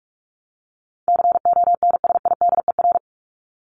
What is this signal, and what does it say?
Morse code, heard as a keyed tone, about 44 wpm, 700 Hz. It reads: CONSIDER
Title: Top 501-750 Words 45wpm